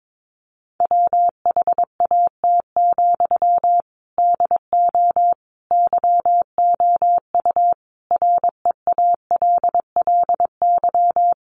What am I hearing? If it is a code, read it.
Morse code, heard as a keyed tone, about 22 wpm, 700 Hz. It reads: WHAT, DO YOU REALLY
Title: Sentences from Top 600 Words 22wpm